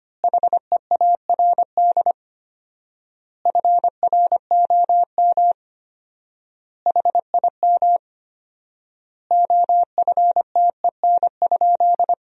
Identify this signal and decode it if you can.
Morse code, heard as a keyed tone, about 25 wpm, 700 Hz. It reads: HEARD FROM HIM OFTEN?
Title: 4x Word-Spacing - Sentences from Top 400 Words - 25wpm